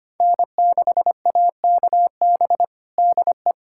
Morse code, heard as a keyed tone, about 25 wpm, 700 Hz. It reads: N6AKB DE
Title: USA Callsign DE Callsign 25wpm